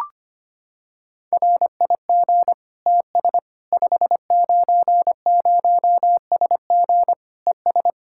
Morse code, heard as a keyed tone, about 25 wpm, 700 Hz. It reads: RIG TS 590SG ES
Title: K1USN SST QSO Practice 25wpm